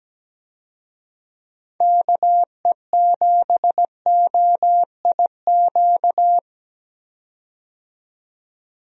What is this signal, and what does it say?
Morse code, heard as a keyed tone, about 17 wpm, 700 Hz. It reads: KE7OIQ
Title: USA Call Signs - 6 Characters - 17wpm